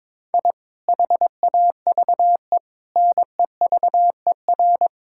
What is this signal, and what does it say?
Morse code, heard as a keyed tone, about 22 wpm, 700 Hz. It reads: I HAVE NEVER
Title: Sentences from Top 1000 Words - Review Part 5 - 22wpm